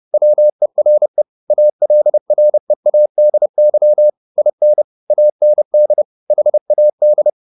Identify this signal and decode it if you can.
Morse code, heard as a keyed tone, about 30 wpm, 600 Hz. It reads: WERE ALREADY IN AND HAD
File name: the old man and the sea / the old man and the sea 1 30wpm